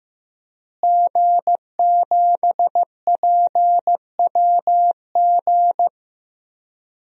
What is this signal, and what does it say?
Morse code, heard as a keyed tone, about 15 wpm, 700 Hz. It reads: G7PWG
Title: DXCC-100 Call Signs - 5 Characters - Repeated 3x - 15wpm